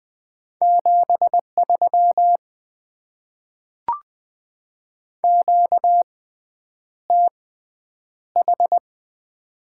Morse code, heard as a keyed tone, about 20 wpm, spaced as for 9 wpm, 700 Hz. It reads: 73 QTH
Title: Top 20 QSO Elements - 6x Character Spacing - 20wpm